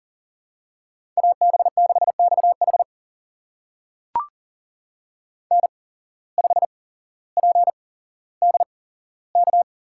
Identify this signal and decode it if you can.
Morse code, heard as a keyed tone, about 40 wpm, spaced as for 17 wpm, 700 Hz. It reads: AB6XH N5PDK
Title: USA Call Signs - 5 Characters - 8x Character Spacing - 40wpm